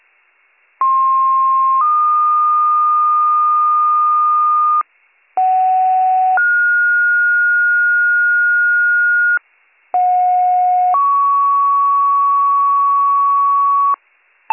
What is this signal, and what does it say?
Two-tone page, heard as a one-second tone, then a three-second tone, about 1060 then 1220 Hz, about 750 then 1430 Hz, about 730 then 1120 Hz. It is heard from 0.8 to 13.9 s.